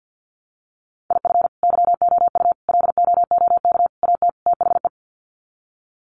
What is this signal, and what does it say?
Morse code, heard as a keyed tone, about 48 wpm, 700 Hz. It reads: IF YOU LOOK AT THE
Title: Sets of 5 Words 50wpm